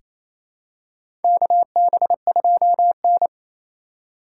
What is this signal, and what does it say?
Morse code, heard as a keyed tone, about 28 wpm, 700 Hz. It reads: KB2N
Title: USA Call Signs - 4 Characters - 3x Character Spacing - 28wpm